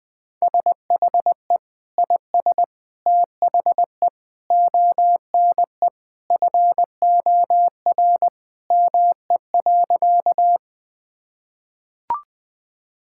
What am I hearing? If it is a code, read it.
Morse code, heard as a keyed tone, about 20 wpm, 700 Hz. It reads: SHE IS THE ONE FOR ME.
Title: Sentences with 2-Word Phrases - 20wpm